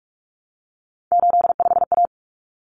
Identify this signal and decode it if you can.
Morse code, heard as a keyed tone, about 44 wpm, 700 Hz. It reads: 85A
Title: Common Cut Numbers - 3-Digit 45wpm